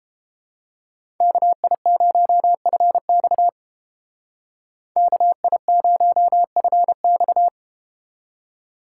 Morse code, heard as a keyed tone, about 32 wpm, 700 Hz. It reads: KI0FX KI0FX